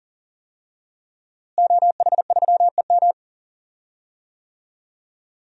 Morse code, heard as a keyed tone, about 40 wpm, 700 Hz. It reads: OH3EM
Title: DXCC-100 Call Signs - 5 Characters - Repeated 3x - 40wpm